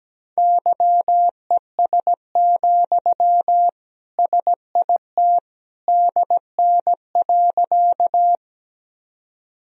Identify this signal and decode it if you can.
Morse code, heard as a keyed tone, about 17 wpm, 700 Hz. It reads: YES, SIT DN.
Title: Sentences from Top 500 Words and Abbreviations 17wpm